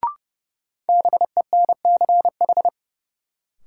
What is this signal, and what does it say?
Morse code, heard as a keyed tone, about 30 wpm, 700 Hz. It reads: BENCH
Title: Sentences from Top 1400 Words - New Word Drill Alt - 30wpm